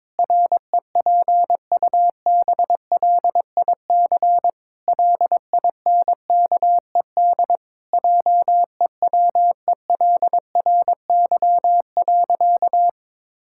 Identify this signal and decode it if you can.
Morse code, heard as a keyed tone, about 22 wpm, 700 Hz. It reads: REPUBLIC LINKED JEWELRY.
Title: Pangram Challenge v2 - 22wpm